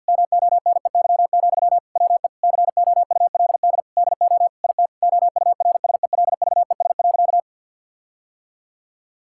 Morse code, heard as a keyed tone, about 50 wpm, 700 Hz. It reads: MONEY, WE COULD DO IT OURSELVES.
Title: Sentences from Top 1600 Words 50wpm